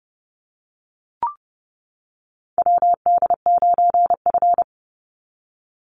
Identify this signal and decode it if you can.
Morse code, heard as a keyed tone, about 30 wpm, 700 Hz. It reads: WD9F
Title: NA SOTA Chaser Callsigns 30wpm